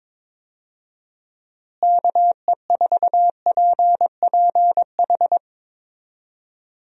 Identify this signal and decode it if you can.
Morse code, heard as a keyed tone, about 22 wpm, 700 Hz. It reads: KE4PPH
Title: USA Call Signs - 6 Characters - Repeated 3x - 22wpm